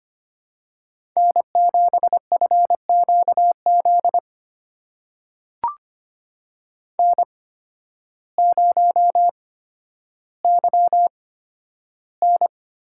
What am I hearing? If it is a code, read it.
Morse code, heard as a keyed tone, about 25 wpm, spaced as for 12 wpm, 700 Hz. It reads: N7FQZ N0YN
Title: USA Call Signs - 8x Character Spacing - 25wpm